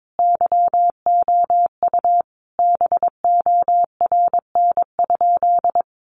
Morse code, heard as a keyed tone, about 22 wpm, 700 Hz. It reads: YOU BORN?
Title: Sentences from Top 1000 Words 22wpm